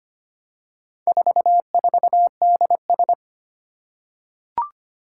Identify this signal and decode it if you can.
Morse code, heard as a keyed tone, about 25 wpm, 700 Hz. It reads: 44DS
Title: Four Challenging Letter-Number Combinations 25wpm